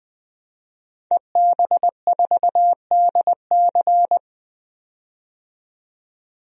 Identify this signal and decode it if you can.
Morse code, heard as a keyed tone, about 20 wpm, 700 Hz. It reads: EB4DC
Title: DXCC-100 Call Signs - 5 Characters - 20wpm